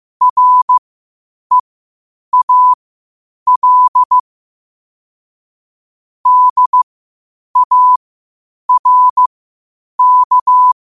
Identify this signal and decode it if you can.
Morse code, heard as a keyed tone, about 15 wpm, spaced as for 8 wpm, 1000 Hz. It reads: REAL DARK
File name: qotdfiles / 8wpm